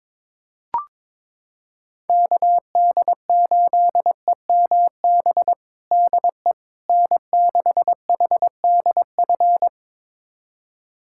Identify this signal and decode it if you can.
Morse code, heard as a keyed tone, about 22 wpm, 700 Hz. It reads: KD8EMB DE N6HDF
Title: USA Callsign DE Callsign 22wpm